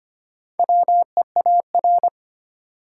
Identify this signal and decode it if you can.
Morse code, heard as a keyed tone, about 25 wpm, 700 Hz. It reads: WEAR
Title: Sentences from Top 1500 Words - New Word Drill Alt - 25wpm